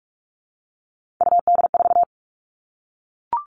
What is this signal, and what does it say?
Morse code, heard as a keyed tone, about 44 wpm, 700 Hz. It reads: UD4
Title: All Cut Numbers - 3-Digit 45wpm